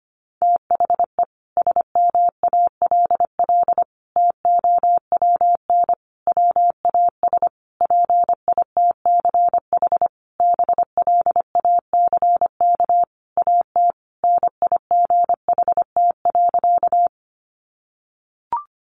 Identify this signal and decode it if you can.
Morse code, heard as a keyed tone, about 25 wpm, 700 Hz. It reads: THE SMALL TOWN WAS PITCH BLACK AT NIGHT.
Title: Sentences from Top 1300 Words 25wpm